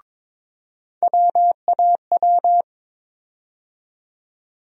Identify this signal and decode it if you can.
Morse code, heard as a keyed tone, about 22 wpm, 700 Hz. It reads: WAW
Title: USA Call Sign Postfixes - 3 Letters 22wpm